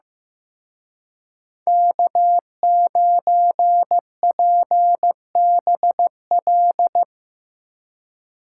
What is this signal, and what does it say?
Morse code, heard as a keyed tone, about 16 wpm, 700 Hz. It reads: K9PBL
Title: Speed Racing - USA 1x3 Call Signs 15wpm